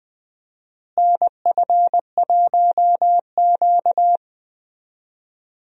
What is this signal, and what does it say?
Morse code, heard as a keyed tone, about 20 wpm, 700 Hz. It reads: NF1Q
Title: USA Call Signs - 4 Characters - Repeated 3x - 20wpm